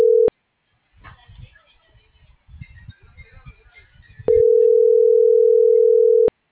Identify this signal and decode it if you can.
Ringback tone (North America), 440 + 480 Hz.
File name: CENTRAL FRALSE / out gerardo2024sub